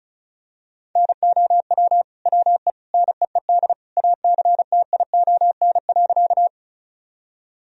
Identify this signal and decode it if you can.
Morse code, heard as a keyed tone, about 36 wpm, 700 Hz. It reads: NOW WE NEED ACTION.
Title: Sentences from Top 1000 Words - Review Part 4 - 35wpm